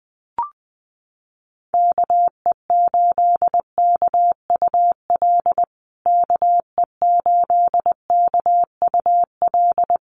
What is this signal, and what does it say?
Morse code, heard as a keyed tone, about 20 wpm, 700 Hz. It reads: KE8KUL KE8KUL
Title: USA Call Signs Repeated 3x 20wpm